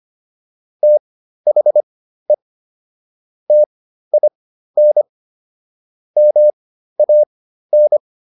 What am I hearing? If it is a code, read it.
Morse code, heard as a keyed tone, about 25 wpm, spaced as for 13 wpm, 600 Hz. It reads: THE TIN MAN